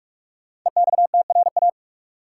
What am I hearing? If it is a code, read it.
Morse code, heard as a keyed tone, about 42 wpm, 700 Hz. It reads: EXTRA
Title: Sentences from Top 1000 Words - New Word Drill - 45wpm